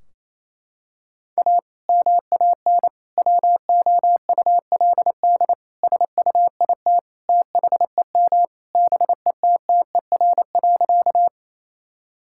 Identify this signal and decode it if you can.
Morse code, heard as a keyed tone, about 27 wpm, 700 Hz. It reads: A MAN WOULD SUIT THEM BETTER.
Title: Sentences from Top 1600 Words 28wpm